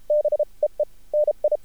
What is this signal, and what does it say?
Morse code, heard as a keyed tone, about 33 wpm, spaced as for 26 wpm, 600 Hz. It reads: BEENI
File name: archived / CRSnet